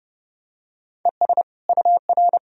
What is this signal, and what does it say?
Morse code, heard as a keyed tone, about 32 wpm, 700 Hz. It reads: ES UR
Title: Sentences from Top 300 Words and Abbreviations 30wpm